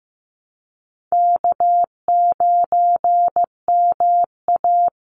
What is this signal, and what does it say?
Morse code, heard as a keyed tone, about 15 wpm, 700 Hz. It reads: K9MA
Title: Call Signs Repeated 3x - 15wpm